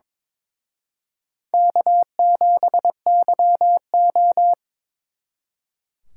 Morse code, heard as a keyed tone, about 26 wpm, 700 Hz. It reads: K7YO
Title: Speed Racing - USA 1x2 Call Signs 22wpm